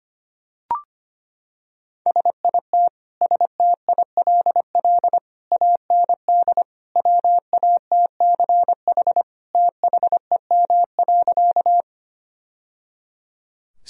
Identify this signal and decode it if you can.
Morse code, heard as a keyed tone, about 25 wpm, 700 Hz. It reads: SIT STILL AND WATCH THEM.
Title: Sentences from Top 1000 Words - Review Part 1 - 25wpm